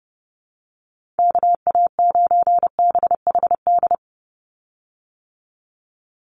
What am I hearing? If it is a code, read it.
Morse code, heard as a keyed tone, about 30 wpm, 700 Hz. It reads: KA9BHD
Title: Call Signs - 30wpm